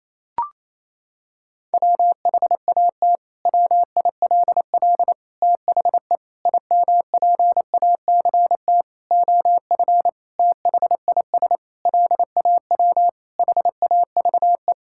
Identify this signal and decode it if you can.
Morse code, heard as a keyed tone, about 27 wpm, 700 Hz. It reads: WHAT WILL THE IMPACT OF THIS LAW HAVE
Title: Sentences from Top 1000 Words - Review Part 6 - 28wpm